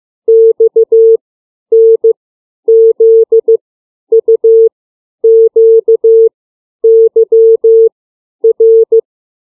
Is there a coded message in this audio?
Yes: Morse code from 0.3 to 9.0 s, heard as a keyed tone, about 15 wpm, 450 Hz. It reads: X N Z U Q Y R